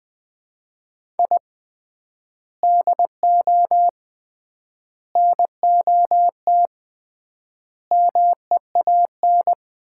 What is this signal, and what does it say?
Morse code, heard as a keyed tone, about 21 wpm, 700 Hz. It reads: I DO NOT MEAN
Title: 3x Word-Spacing - Sets of 5 Words - 20wpm